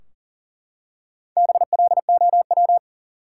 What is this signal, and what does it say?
Morse code, heard as a keyed tone, about 40 wpm, 700 Hz. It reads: BLOW
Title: Four Letter Words - 40wpm